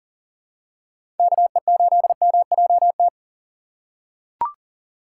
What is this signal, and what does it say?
Morse code, heard as a keyed tone, about 40 wpm, 700 Hz. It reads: KE8MJT
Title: USA Call Signs - 6 Characters - 2x Character Spacing - 40wpm